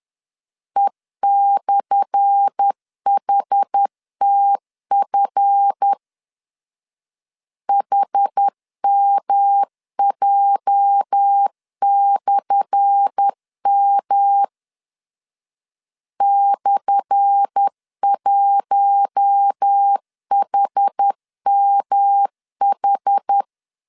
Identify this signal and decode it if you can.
Morse code, heard as a keyed tone, about 10 wpm, 800 Hz. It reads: E/HTF HMJ/M /1HMH
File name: LEZIONI / 10 SIGNS / 10A SLASH 10wpm 2808